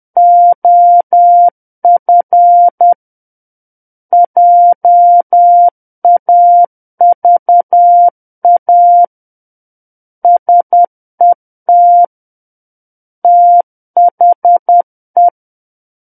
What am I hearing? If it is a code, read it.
Morse code, heard as a keyed tone, about 10 wpm, 700 Hz. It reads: OF JAVA SET THE
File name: Book One / 01-Chapter 1 - 10 WPM